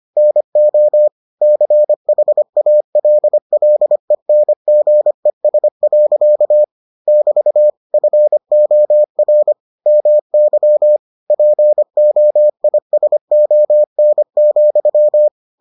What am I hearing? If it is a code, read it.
Morse code, heard as a keyed tone, about 25 wpm, 600 Hz. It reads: NO CHALLENGES. <BT> FOR MY POISON,